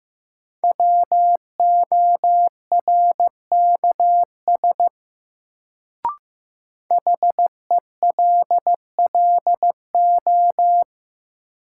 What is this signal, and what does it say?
Morse code, heard as a keyed tone, about 15 wpm, 700 Hz. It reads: WORKS HELLO
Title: Sentences from Top 1100 Words - New Word Drill Alt - 15wpm